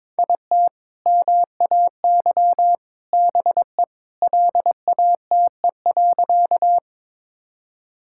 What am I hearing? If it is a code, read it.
Morse code, heard as a keyed tone, about 22 wpm, 700 Hz. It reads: IT MAY BE LATE.
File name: practice / 22wpm / Sentences with 3-Word Phrases - 22wpm